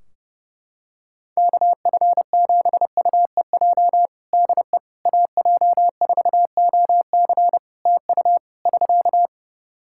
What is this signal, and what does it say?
Morse code, heard as a keyed tone, about 30 wpm, 700 Hz. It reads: KF7UEJ DE AJ4OC TU <SK>
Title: Practice QSO - Exchange 4 - Ending - 30wpm